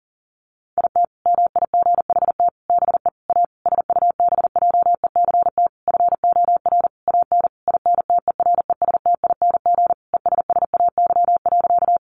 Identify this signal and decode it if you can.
Morse code, heard as a keyed tone, about 40 wpm, 700 Hz. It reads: IT MIGHT BE A SUBJECT FOR AN INTERESTING ESSAY.
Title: Sentences from Top 1500 Words 40wpm